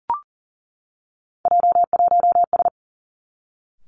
Morse code, heard as a keyed tone, about 38 wpm, 690 Hz. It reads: J1S
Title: Three Challenging Letter-Number Combinations 40wpm